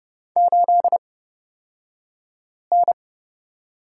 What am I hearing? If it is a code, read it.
Morse code, heard as a keyed tone, about 30 wpm, spaced as for 5 wpm, 700 Hz. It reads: 8N
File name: Lesson 38d - Letter q - Two Characters 30wpm